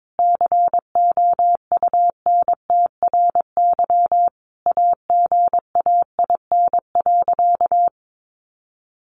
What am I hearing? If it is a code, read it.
Morse code, heard as a keyed tone, about 22 wpm, 700 Hz. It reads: COUNTRY AGAIN.
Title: Sentences from Top 1200 Words 22wpm